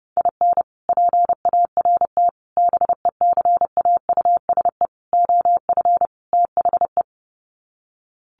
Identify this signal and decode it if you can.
Morse code, heard as a keyed tone, about 28 wpm, 700 Hz. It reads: IN PART BECAUSE OF THE
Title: Sets of 5 Words Encore2 30wpm